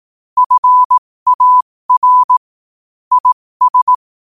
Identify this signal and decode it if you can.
Morse code, heard as a keyed tone, about 18 wpm, spaced as for 15 wpm, 1000 Hz. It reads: FAR IS
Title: 15 WPM morse code quotes for Fri, 01 Mar 2024